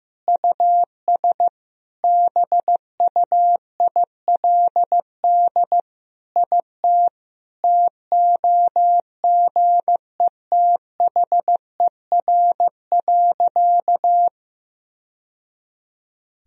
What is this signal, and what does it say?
Morse code, heard as a keyed tone, about 15 wpm, 700 Hz. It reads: US BUILD IT TOGETHER.
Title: Sentences from Top 1000 Words - Review Part 1 - 15wpm